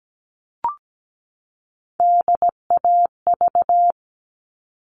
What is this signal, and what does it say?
Morse code, heard as a keyed tone, about 17 wpm, 700 Hz. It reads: DAV
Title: All Cut Numbers - 3-Digit 17wpm